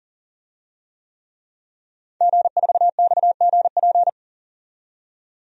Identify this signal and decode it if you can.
Morse code, heard as a keyed tone, about 40 wpm, 700 Hz. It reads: G4XGP
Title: DXCC-100 Call Signs - 5 Characters - 40wpm